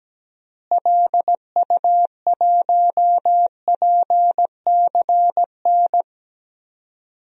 Morse code, heard as a keyed tone, about 17 wpm, 700 Hz. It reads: LU1PCN
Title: DXCC-100 Call Signs - 6 Characters - Repeated 3x - 17wpm